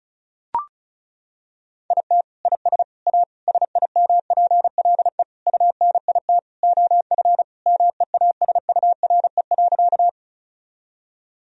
Morse code, heard as a keyed tone, about 34 wpm, 700 Hz. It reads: IT IS A SIMPLE UNIT OF MEASURE.
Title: Sentences from Top 1000 Words - Review Part 2 - 35wpm